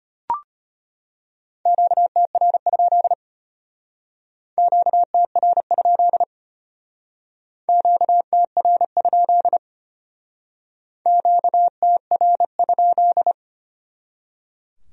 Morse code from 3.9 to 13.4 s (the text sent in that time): QTR? QTR? QTR?